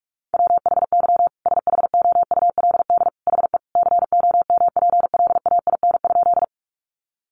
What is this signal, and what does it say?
Morse code, heard as a keyed tone, about 44 wpm, 700 Hz. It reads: WHY SHOULD HE COMPLAIN?
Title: Sentences from Top 1300 Words 45wpm